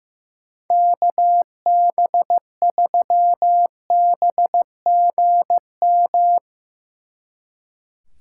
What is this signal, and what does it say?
Morse code, heard as a keyed tone, about 16 wpm, 700 Hz. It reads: KB3BGM
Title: Speed Racing - USA 2x3 Call Signs 15wpm